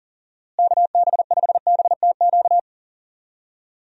Morse code, heard as a keyed tone, about 40 wpm, 700 Hz. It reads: KB5BTQ
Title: USA Call Signs - 6 Characters - Repeated 3x - 40wpm